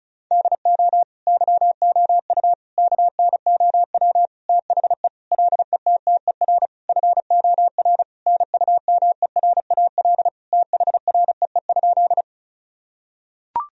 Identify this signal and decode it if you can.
Morse code, heard as a keyed tone, about 34 wpm, 700 Hz. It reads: DO YOU KNOW THE LETTER FOR NUMERAL THREE?
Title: Sentences from Top 1000 Words - Review Part 2 - 35wpm